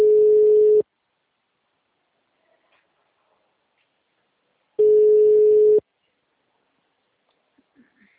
Ringback tone (Europe, 425 Hz).